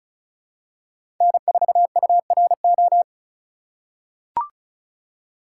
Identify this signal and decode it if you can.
Morse code, heard as a keyed tone, about 36 wpm, 700 Hz. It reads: N4URO
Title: USA Call Signs - 5 Characters - 4x Character Spacing - 35wpm